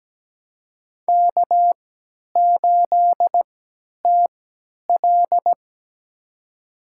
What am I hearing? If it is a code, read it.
Morse code, heard as a keyed tone, about 17 wpm, spaced as for 10 wpm, 700 Hz. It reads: K8TL
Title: USA Call Signs - 3x Character Spacing - 17wpm